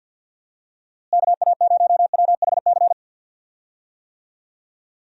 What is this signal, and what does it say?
Morse code, heard as a keyed tone, about 52 wpm, 700 Hz. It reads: KA0WHC
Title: USA Call Signs - 6 Characters - Repeated 2x - 50wpm